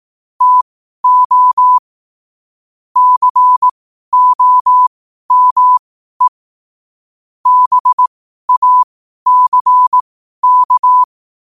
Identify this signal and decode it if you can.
Morse code, heard as a keyed tone, about 18 wpm, spaced as for 12 wpm, 1000 Hz. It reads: TO COME BACK
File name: 2026-01-07-12wpm